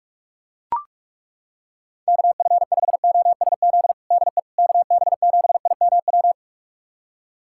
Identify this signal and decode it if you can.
Morse code, heard as a keyed tone, about 42 wpm, 700 Hz. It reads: KF5OSZ DE KB7IMW
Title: USA Callsign DE Callsign 45wpm